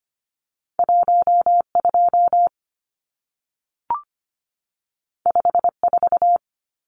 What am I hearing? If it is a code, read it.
Morse code, heard as a keyed tone, about 25 wpm, 700 Hz. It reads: 12 54
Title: Numbers - 2-Digit 25wpm